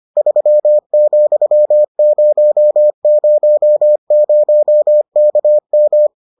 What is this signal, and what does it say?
Morse code, heard as a keyed tone, about 25 wpm, 600 Hz. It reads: 3,000KM